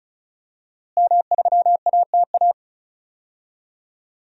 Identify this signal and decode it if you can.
Morse code, heard as a keyed tone, about 34 wpm, 700 Hz. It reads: M3ATA
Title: DXCC-100 Call Signs - 5 Characters - Repeated 3x - 35wpm